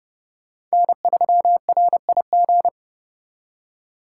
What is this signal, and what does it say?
Morse code, heard as a keyed tone, about 36 wpm, 700 Hz. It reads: N3RIG
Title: Speed Racing - USA 1x3 Call Signs 30wpm